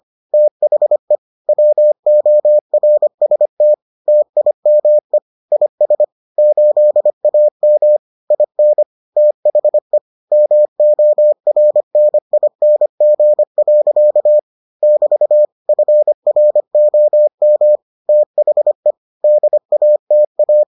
Morse code, heard as a keyed tone, about 25 wpm, 600 Hz. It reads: THE WORST TIME IS 8AM IN THE MORNING. <BT> FROM THE DATA